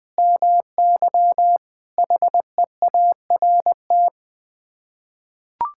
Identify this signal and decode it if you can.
Morse code, heard as a keyed tone, about 20 wpm, 700 Hz. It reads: MY HEART
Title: Sets of 9 Words - 20wpm